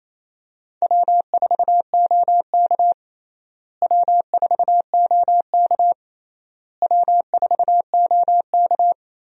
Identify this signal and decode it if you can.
Morse code, heard as a keyed tone, about 28 wpm, 700 Hz. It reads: W4OK W4OK W4OK
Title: USA Call Signs - 4 Characters - Repeated 3x - 28wpm